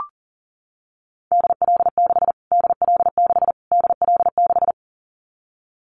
Morse code, heard as a keyed tone, about 40 wpm, 700 Hz. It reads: DL6 DL6 DL6
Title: Callsign Prefixes for Top 25 DXCC - Repeated 3x - 40wpm